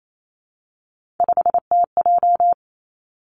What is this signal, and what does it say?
Morse code, heard as a keyed tone, about 28 wpm, 690 Hz. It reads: 5TJ
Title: Three Challenging Letter-Number Combinations 28wpm